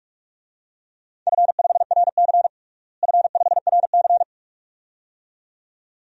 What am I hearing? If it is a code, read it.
Morse code, heard as a keyed tone, about 41 wpm, 700 Hz. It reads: F5RC F5RC
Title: DXCC-100 Call Signs - 4 Characters - Repeated 2x - 45wpm